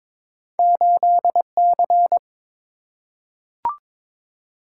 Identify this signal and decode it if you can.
Morse code, heard as a keyed tone, about 22 wpm, 700 Hz. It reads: 8C
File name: practice / 22wpm / Two Letter-Number Combinations - 22wpm